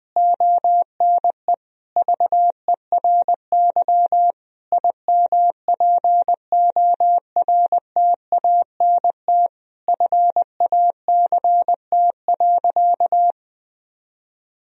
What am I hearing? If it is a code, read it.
Morse code, heard as a keyed tone, about 20 wpm, 700 Hz. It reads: ONE VERY IMPORTANT FACT.
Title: Sentences from Top 1000 Words - Review Part 3 - 20wpm